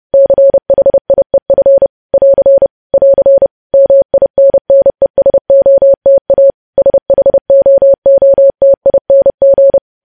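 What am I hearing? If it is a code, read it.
Morse code, heard as a keyed tone, about 30 wpm, 550 Hz. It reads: CHIEF <AR> <AR> MINNESOTA SHOOTING
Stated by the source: News headlines sent in Morse at 30.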